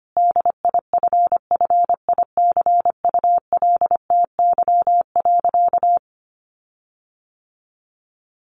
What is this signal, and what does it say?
Morse code, heard as a keyed tone, about 25 wpm, 700 Hz. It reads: DIFFICULTY.
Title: Sentences from Top 1500 Words 25wpm